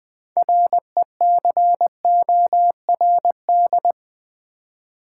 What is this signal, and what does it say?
Morse code, heard as a keyed tone, about 20 wpm, 700 Hz. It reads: RECORD
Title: Six Letter Words - 20wpm